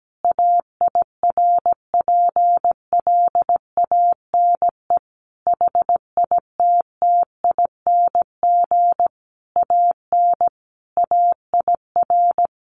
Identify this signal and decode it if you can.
Morse code, heard as a keyed tone, about 17 wpm, 700 Hz. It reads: AIRPLANE HITTING AN AIR
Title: A Sound Of Thunder 17wpm